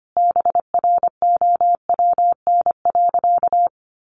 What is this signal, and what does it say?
Morse code, heard as a keyed tone, about 25 wpm, 700 Hz. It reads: BROWN.
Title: Sentences from Top 1200 Words 25wpm